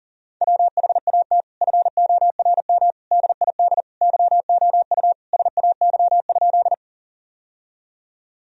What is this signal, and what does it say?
Morse code, heard as a keyed tone, about 40 wpm, 700 Hz. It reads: WHAT FORM DID YOU SAY?
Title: Sentences from Top 1000 Words - Review Part 1 - 40wpm